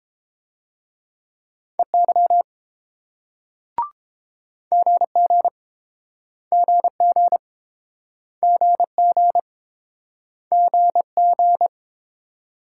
Morse code, heard as a keyed tone, about 31 wpm, 700 Hz. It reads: EY GG GG GG GG